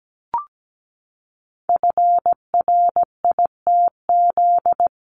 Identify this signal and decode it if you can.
Morse code, heard as a keyed tone, about 17 wpm, 700 Hz. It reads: FRITZ
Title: CWT Exchange 17wpm